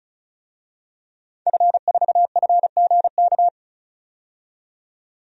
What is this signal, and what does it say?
Morse code, heard as a keyed tone, about 34 wpm, 700 Hz. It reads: F4FGK
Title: DXCC-100 Call Signs - 5 Characters - 35wpm